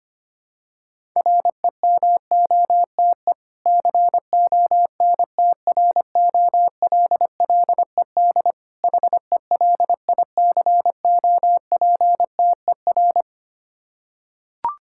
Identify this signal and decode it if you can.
Morse code, heard as a keyed tone, about 25 wpm, 700 Hz. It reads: REMOTE CONTROLLED HELICOPTER
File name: practice / 25wpm / Difficult Words with Hints - 25wpm